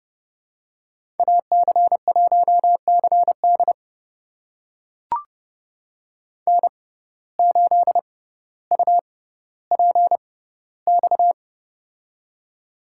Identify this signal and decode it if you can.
Morse code, heard as a keyed tone, about 30 wpm, spaced as for 15 wpm, 700 Hz. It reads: AC1CD N8UPX
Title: USA Call Signs - 5 Characters - 6x Character Spacing - 30wpm